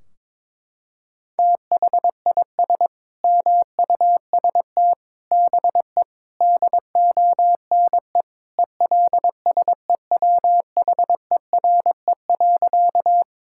Morse code, heard as a keyed tone, about 22 wpm, 700 Hz. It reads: THIS MUST BE DONE ELSEWHERE.
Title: Sentences from Top 1600 Words 22wpm